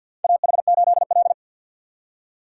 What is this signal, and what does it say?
Morse code, heard as a keyed tone, about 53 wpm, 700 Hz. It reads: AH8L
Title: USA Call Signs - 4 Characters - 6x Character Spacing - 50wpm